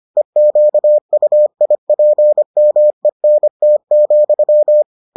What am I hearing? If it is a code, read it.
Morse code, heard as a keyed tone, about 25 wpm, 600 Hz. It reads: EQUIPMENT,